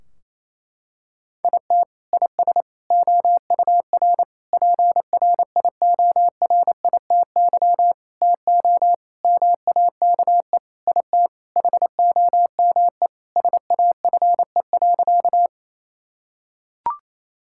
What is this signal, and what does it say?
Morse code, heard as a keyed tone, about 28 wpm, 700 Hz. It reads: IT IS OUR PRIORITY TO MAKE IT HOME SAFE.
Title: Sentences from Top 1600 Words 28wpm